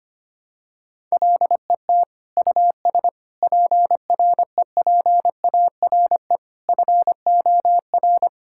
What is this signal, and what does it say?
Morse code, heard as a keyed tone, about 25 wpm, 700 Hz. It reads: LET US PREPARE FOR
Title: Sentences from Top 1000 Words - Review Part 6 - 25wpm